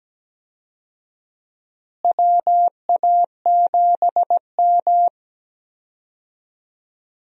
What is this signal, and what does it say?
Morse code, heard as a keyed tone, about 17 wpm, 700 Hz. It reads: WA7M
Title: USA Call Signs - 4 Characters - Repeated 2x - 17wpm